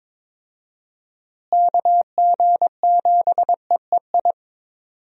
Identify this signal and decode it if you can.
Morse code, heard as a keyed tone, about 22 wpm, 700 Hz. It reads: KG7EEI
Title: USA Call Signs - 6 Characters - Repeated 2x - 22wpm